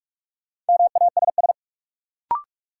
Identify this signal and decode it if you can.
Morse code, heard as a keyed tone, about 45 wpm, 700 Hz. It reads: MASS
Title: Sets of 5 Words Encore - New Word Drill - 45wpm